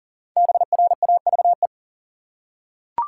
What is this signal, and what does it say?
Morse code, heard as a keyed tone, about 37 wpm, 700 Hz. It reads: BRAVE
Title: Five Letter Words - 40wpm